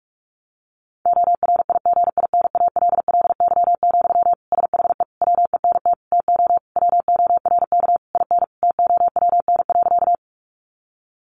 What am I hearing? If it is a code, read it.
Morse code, heard as a keyed tone, about 45 wpm, 700 Hz. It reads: ORIGINALLY, SHE WENT TO WORK IN TOWN.
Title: Sentences from Top 1500 Words 45wpm